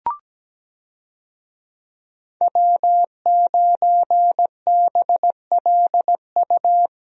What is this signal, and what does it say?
Morse code, heard as a keyed tone, about 17 wpm, 700 Hz. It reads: W9BLU
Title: USA Call Signs - 5 Characters - Repeated 2x - 17wpm